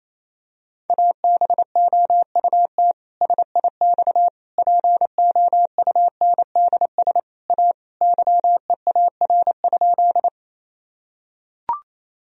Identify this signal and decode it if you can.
Morse code, heard as a keyed tone, about 28 wpm, 700 Hz. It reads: ABOUT SIX POUNDS A YEAR?
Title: Sentences from Top 1000 Words - Review Part 2 - 28wpm